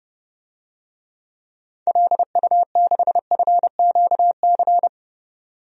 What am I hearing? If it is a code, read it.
Morse code, heard as a keyed tone, about 31 wpm, 700 Hz. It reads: LU6FQC
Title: DXCC-100 Call Signs - 6 Characters - 30wpm